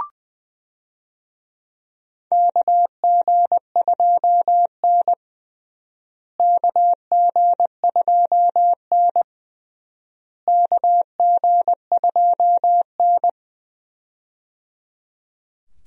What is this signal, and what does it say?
Morse code, heard as a keyed tone, about 20 wpm, 700 Hz. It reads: KG2N KG2N KG2N
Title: USA Call Signs - 4 Characters - Repeated 3x - 20wpm